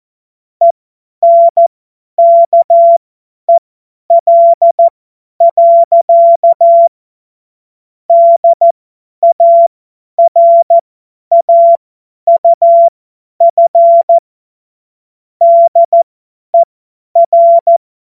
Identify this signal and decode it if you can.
Morse code, heard as a keyed tone, about 14 wpm, spaced as for 10 wpm, 680 Hz. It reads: ENKEL. DARAUF DER
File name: Morsekurs / Witze / Witz 27 Februar 2024 50 F70